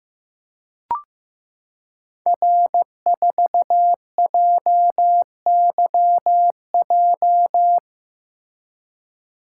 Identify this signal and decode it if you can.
Morse code, heard as a keyed tone, about 15 wpm, 700 Hz. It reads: R4JYJ
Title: Code Groups - 5 Letter:Numbers - 15wpm